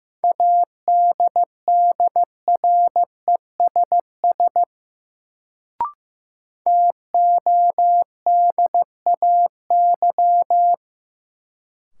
Morse code, heard as a keyed tone, about 15 wpm, 700 Hz. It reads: ADDRESS TODAY
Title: Sets of 5 Words Encore - New Word Drill - 15wpm